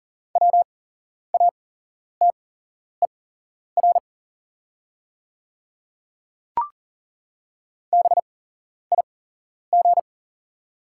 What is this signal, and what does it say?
Morse code, heard as a keyed tone, about 40 wpm, spaced as for 11 wpm, 690 Hz. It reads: WATER BIG
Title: 8x Character Spacing - Top 200 Words - 40wpm